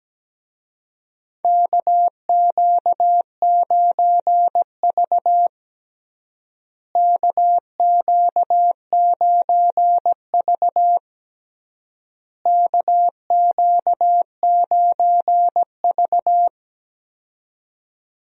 Morse code, heard as a keyed tone, about 17 wpm, 700 Hz. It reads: KQ9V KQ9V KQ9V
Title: USA Call Signs - 4 Characters - Repeated 3x - 17wpm